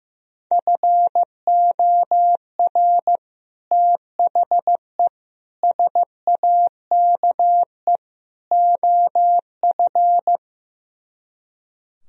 Morse code, heard as a keyed tone, about 15 wpm, 700 Hz. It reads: FOR THE SAKE OF
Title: Speed-Racing - Sets of 4 Words 15wpm